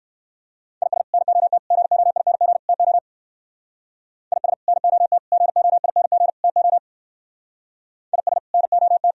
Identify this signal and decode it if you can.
Morse code, heard as a keyed tone, about 59 wpm, 700 Hz. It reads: IS NOT GOING TO IS NOT GOING TO IS NOT